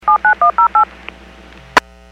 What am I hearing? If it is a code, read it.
DTMF, heard as telephone touch tones, tones of about 110 ms. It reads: 79108
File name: WPS / 04194 06-26 WPS 451.25 WPS